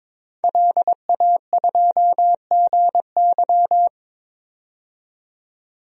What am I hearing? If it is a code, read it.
Morse code, heard as a keyed tone, about 21 wpm, 700 Hz. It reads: LA2GY
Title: DXCC-100 Call Signs - 5 Characters - 22wpm